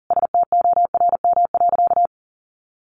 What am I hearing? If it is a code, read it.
Morse code, heard as a keyed tone, about 40 wpm, 700 Hz. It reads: STORM.
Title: Sentences from Top 1500 Words 40wpm